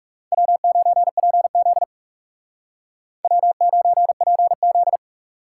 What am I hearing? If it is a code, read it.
Morse code, heard as a keyed tone, about 38 wpm, 700 Hz. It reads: W9PZ W9PZ